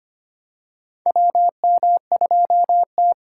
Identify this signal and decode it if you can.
Morse code, heard as a keyed tone, about 25 wpm, 700 Hz. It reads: WM2T
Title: USA Call Signs - 4 Characters - 3x Character Spacing - 25wpm